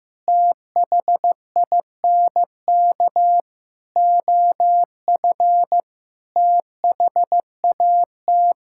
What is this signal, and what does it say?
Morse code, heard as a keyed tone, about 15 wpm, 700 Hz. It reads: THINK OF THAT
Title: Sentences from Top 1000 Words - Review Part 6 - 15wpm